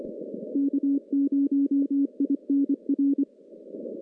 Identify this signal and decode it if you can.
Morse code, heard as a keyed tone, about 25 wpm, 280 Hz. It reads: K0INR
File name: contests / CW-SS-k5tr- -0600